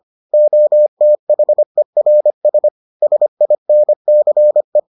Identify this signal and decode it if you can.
Morse code, heard as a keyed tone, about 25 wpm, 600 Hz. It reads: OTHERS SINCE